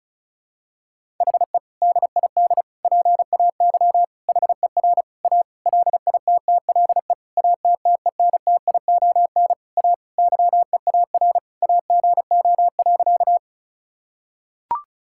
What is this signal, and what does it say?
Morse code, heard as a keyed tone, about 36 wpm, 700 Hz. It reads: HE DID PAY HER A LITTLE ATTENTION A YEAR AGO.
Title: Sentences from Top 1000 Words - Review Part 6 - 35wpm